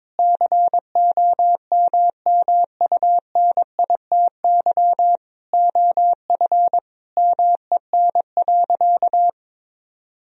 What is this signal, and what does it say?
Morse code, heard as a keyed tone, about 22 wpm, 700 Hz. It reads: COMMUNITY OF MEN.
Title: Sentences from Top 1000 Words - Review Part 4 - 22wpm